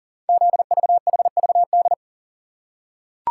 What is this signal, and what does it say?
Morse code, heard as a keyed tone, about 38 wpm, 690 Hz. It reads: ZVHVD
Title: Code Groups - 5 Letters - 40wpm